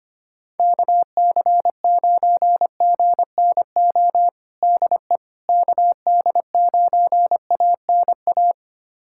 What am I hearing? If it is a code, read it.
Morse code, heard as a keyed tone, about 25 wpm, 700 Hz. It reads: KC9GNO DE KD9ANA
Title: USA Callsign DE Callsign - 6 Characters - 25wpm